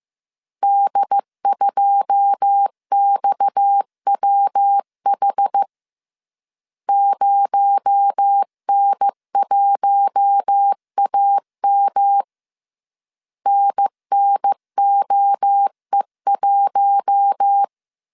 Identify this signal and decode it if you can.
Morse code, heard as a keyed tone, about 15 wpm, 800 Hz. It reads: D2XWH 0N1AM NNOE1
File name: CW / DET / 1 9 1500C S C 14wpm 3245